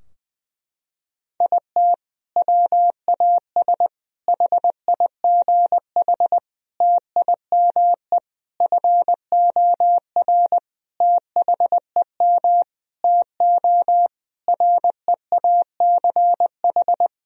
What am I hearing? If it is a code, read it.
Morse code, heard as a keyed tone, about 20 wpm, 700 Hz. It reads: IT WAS HIGH TIME FOR THEM TO REACH
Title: Sentences from Top 1000 Words - Review Part 2 - 20wpm